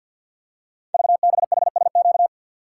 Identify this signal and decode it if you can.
Morse code, heard as a keyed tone, about 53 wpm, 700 Hz. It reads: VBHSQ
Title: Code Groups - 5 Letter:Numbers - 50wpm